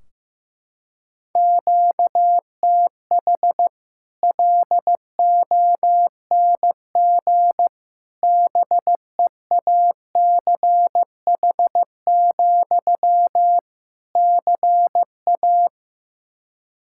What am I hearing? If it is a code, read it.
Morse code, heard as a keyed tone, about 15 wpm, 700 Hz. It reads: QTH LONG BEACH, CA
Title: Practice QSO - Exchange 1 - Beginning - 15wpm